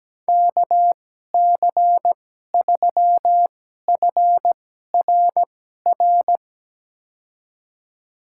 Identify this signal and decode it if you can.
Morse code, heard as a keyed tone, about 17 wpm, spaced as for 12 wpm, 700 Hz. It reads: KC3FRR
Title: USA Call Signs - 2x Character Spacing - 17wpm